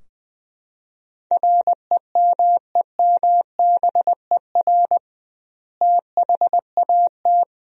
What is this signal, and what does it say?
Morse code, heard as a keyed tone, about 21 wpm, 700 Hz. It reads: REMEMBER THAT
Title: 2x Word-Spacing - Sentences from Top 400 Words - 20wpm